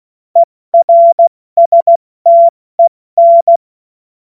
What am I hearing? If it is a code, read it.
Morse code, heard as a keyed tone, about 16 wpm, 680 Hz. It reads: ERSTEN
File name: Morsekurs / Witze / Witz 15. Februar 2022 70 F80